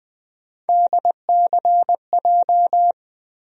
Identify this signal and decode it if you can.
Morse code, heard as a keyed tone, about 20 wpm, 700 Hz. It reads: DCJ
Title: USA Call Sign Postfixes - 3 Letters 20wpm